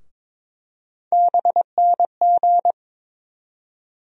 Morse code, heard as a keyed tone, about 22 wpm, 700 Hz. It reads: BNG
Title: USA Call Sign Postfixes - 3 Letters 22wpm